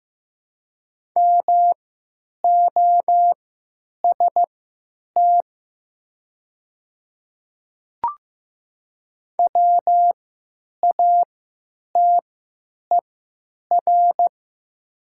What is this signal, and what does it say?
Morse code, heard as a keyed tone, about 15 wpm, spaced as for 8 wpm, 700 Hz. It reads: MOST WATER
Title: 3x Character Spacing - Top 500 Words - 15wpm